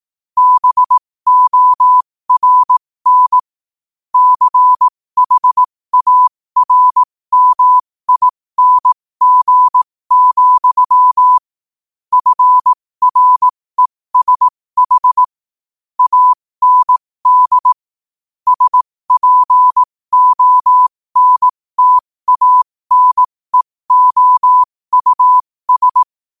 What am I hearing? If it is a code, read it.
Morse code, heard as a keyed tone, about 18 wpm, spaced as for 15 wpm, 1000 Hz. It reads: BORN CHARMING, FRESH AND SPONTANEOUS
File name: qotdfiles / 15wpm